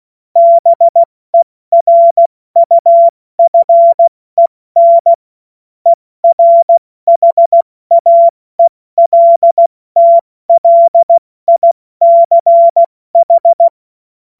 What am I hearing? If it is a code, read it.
Morse code, heard as a keyed tone, about 16 wpm, 680 Hz. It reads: BERUFEN ERHAELTLICH
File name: Morsekurs / Witze / Witz 07. Juni 2022 70 F80